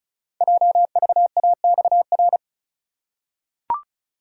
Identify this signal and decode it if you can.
Morse code, heard as a keyed tone, about 35 wpm, 700 Hz. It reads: JVAXR
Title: Code Groups - 5 Letter:Numbers - 35wpm